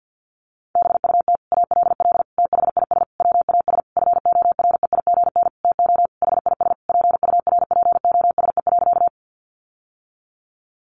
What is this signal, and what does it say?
Morse code, heard as a keyed tone, about 49 wpm, 700 Hz. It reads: BUT ALL THIS WAS FOREIGN TO HIS PURPOSE.
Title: Sentences from Top 1400 Words 50wpm